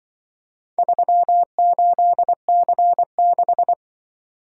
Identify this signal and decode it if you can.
Morse code, heard as a keyed tone, about 23 wpm, 700 Hz. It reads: 38C6
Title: Speed-Racing - Four Letter-Number Combinations - 20wpm